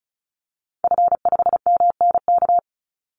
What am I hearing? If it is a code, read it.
Morse code, heard as a keyed tone, about 34 wpm, 700 Hz. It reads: F5MNK
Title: Call Signs - 35wpm